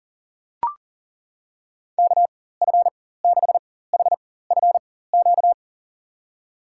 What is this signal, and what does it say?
Morse code, heard as a keyed tone, about 39 wpm, spaced as for 19 wpm, 700 Hz. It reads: KF6HFQ
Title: USA Call Signs - 6 Characters - 4x Character Spacing - 40wpm